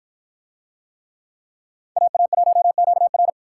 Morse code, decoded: AA1QR

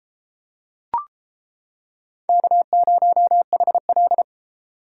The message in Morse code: K0HL

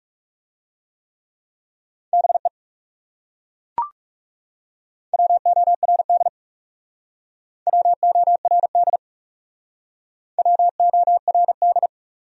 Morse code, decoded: BE WORD WORD WORD

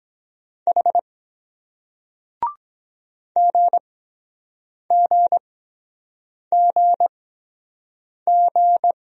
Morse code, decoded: H GGGG

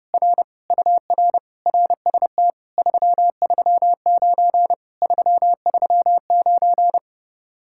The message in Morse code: R UR RST 339 339

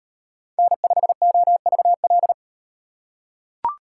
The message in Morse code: N5OVL

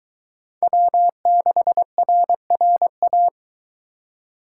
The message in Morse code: W6RRA

